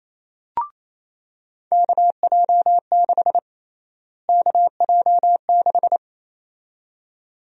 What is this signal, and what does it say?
Morse code, heard as a keyed tone, about 28 wpm, 700 Hz. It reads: KJ6 KJ6
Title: USA Call Sign Prefixes - Repeated 2x - 28wpm